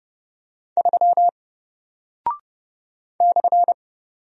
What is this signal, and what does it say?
Morse code, heard as a keyed tone, about 30 wpm, 700 Hz. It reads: 3 /
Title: Single Character (Letter:Number:Punctuation:Prosign) Alt 30wpm